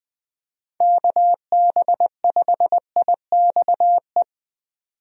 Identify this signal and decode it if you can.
Morse code, heard as a keyed tone, about 21 wpm, 700 Hz. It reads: KB5IXE
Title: USA Call Signs - 6 Characters - 20wpm